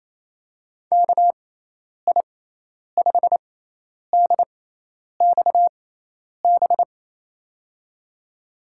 Morse code, heard as a keyed tone, about 27 wpm, spaced as for 10 wpm, 700 Hz. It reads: KI5DXB
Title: USA Call Signs - 6 Characters - 6x Character Spacing - 28wpm